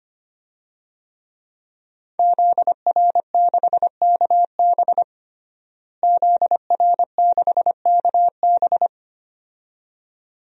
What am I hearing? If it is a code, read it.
Morse code, heard as a keyed tone, about 25 wpm, 700 Hz. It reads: ZR6KB ZR6KB
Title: DXCC-100 Call Signs - 5 Characters - Repeated 2x - 25wpm